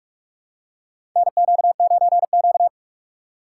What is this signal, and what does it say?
Morse code, heard as a keyed tone, about 42 wpm, 700 Hz. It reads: NQ9Q
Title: USA Call Signs - 4 Characters - 2x Character Spacing - 45wpm